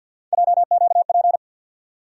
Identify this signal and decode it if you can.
Morse code, heard as a keyed tone, about 48 wpm, 700 Hz. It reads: JQP
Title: USA Call Sign Postfixes - 3 Letters 50wpm